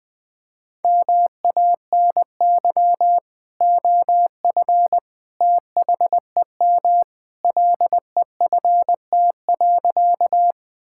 Morse code, heard as a keyed tone, about 20 wpm, 700 Hz. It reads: MANY OF THEM LEFT.
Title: Sentences with 2-Word Phrases - 20wpm